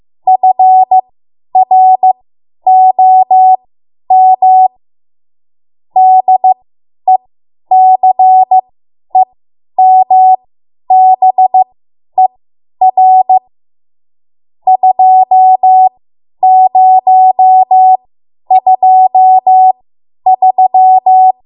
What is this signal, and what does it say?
Morse code, heard as a keyed tone, about 15 wpm, spaced as for 10 wpm, 750 Hz. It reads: FROM DECEMBER 2023
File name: Morse / 10WPM